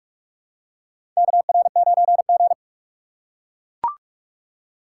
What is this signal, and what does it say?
Morse code, heard as a keyed tone, about 43 wpm, 700 Hz. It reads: KR9G
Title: USA Call Signs - 4x Character Spacing - 45wpm